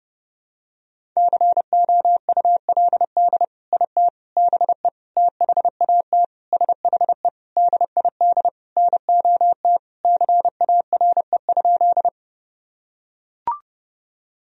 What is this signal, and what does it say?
Morse code, heard as a keyed tone, about 30 wpm, 700 Hz. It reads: COULD IT BE THAT SHE DID NOT CARE?
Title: Sentences from Top 1000 Words - Review Part 1 - 30wpm